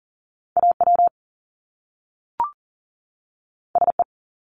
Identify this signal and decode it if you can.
Morse code, heard as a keyed tone, about 40 wpm, 700 Hz. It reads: AW SE
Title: Top Two Letter Combinations in Words 40wpm